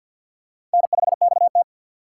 Morse code, heard as a keyed tone, about 50 wpm, 700 Hz. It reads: N5XT
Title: USA Call Signs - 4 Characters - 3x Character Spacing - 50wpm